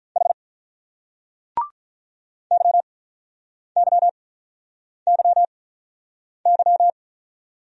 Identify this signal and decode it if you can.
Morse code, heard as a keyed tone, about 44 wpm, 700 Hz. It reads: H Y Y Y Y